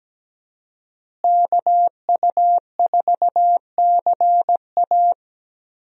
Morse code, heard as a keyed tone, about 17 wpm, 700 Hz. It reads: KU4CA
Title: USA Call Signs - 5 Characters - Repeated 3x - 17wpm